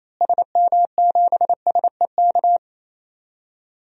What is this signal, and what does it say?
Morse code, heard as a keyed tone, about 28 wpm, 700 Hz. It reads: SM7SEK
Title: DXCC-100 Call Signs - 6 Characters - Repeated 3x - 28wpm